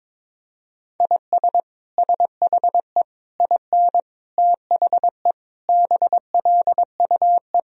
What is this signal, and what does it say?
Morse code, heard as a keyed tone, about 22 wpm, 700 Hz. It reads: IS SHE IN THE BLUE
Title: Sentences from Top 1000 Words - Review Part 3 - 22wpm